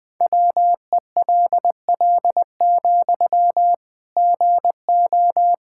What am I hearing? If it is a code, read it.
Morse code, heard as a keyed tone, about 20 wpm, 700 Hz. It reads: WELL, GO
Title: Sentences from Top 1600 Words 20wpm